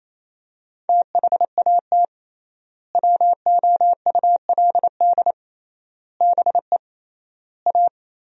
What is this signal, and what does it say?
Morse code, heard as a keyed tone, about 28 wpm, 700 Hz. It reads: THAT WOULD BE A
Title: 3x Word-Spacing - Sentences from Top 500 Words - 28wpm